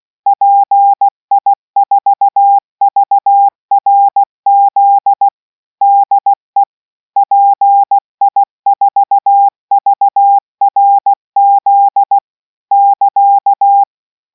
Morse code, heard as a keyed tone, about 16 wpm, 800 Hz. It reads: PI4VRZ DE PI4VRZ <KA>